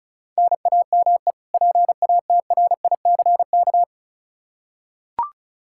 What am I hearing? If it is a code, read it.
Morse code, heard as a keyed tone, about 36 wpm, 700 Hz. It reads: NAME PATRICK
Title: Practice QSO - Exchange 1 - Beginning - 35wpm